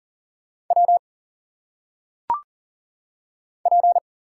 Morse code, heard as a keyed tone, about 38 wpm, 700 Hz. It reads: W P
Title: Single Letters - Phonetic Pronunciation 40wpm